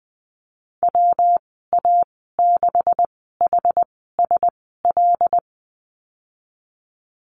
Morse code, heard as a keyed tone, about 20 wpm, spaced as for 15 wpm, 700 Hz. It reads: WA6HSL
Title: 161 Lesson 26g - Letter g - Call Signs 20wpm